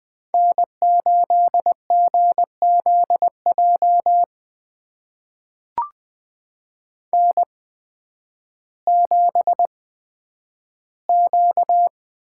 Morse code, heard as a keyed tone, about 20 wpm, spaced as for 11 wpm, 700 Hz. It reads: N8GZJ N7Q
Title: USA Call Signs - 5 Characters - 8x Character Spacing - 20wpm